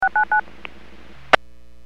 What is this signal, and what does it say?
DTMF, heard as telephone touch tones, tones of about 100 ms. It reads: 699